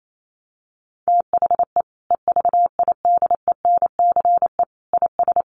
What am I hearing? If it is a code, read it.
Morse code, heard as a keyed tone, about 28 wpm, 700 Hz. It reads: THE EVIDENCE IS
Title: Sentences from Top 1300 Words 28wpm